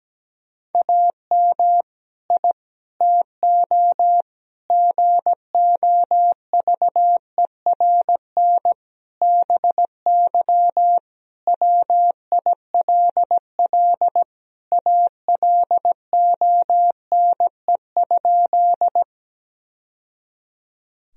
Morse code, heard as a keyed tone, about 17 wpm, 700 Hz. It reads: AM I TO GOVERN BY WILL ALONE?
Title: Sentences from Top 1000 Words - Review Part 2 - 17wpm